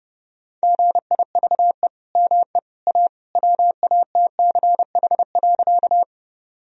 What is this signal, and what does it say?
Morse code, heard as a keyed tone, about 30 wpm, 700 Hz. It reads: GIVE ME A WATCH.
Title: Sentences from Top 1000 Words - Review Part 1 - 30wpm